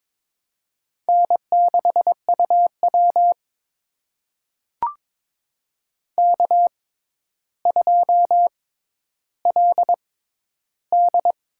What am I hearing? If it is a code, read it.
Morse code, heard as a keyed tone, about 23 wpm, spaced as for 11 wpm, 700 Hz. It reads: N6UW K2LD
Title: USA Call Signs - 4 Characters - 6x Character Spacing - 22wpm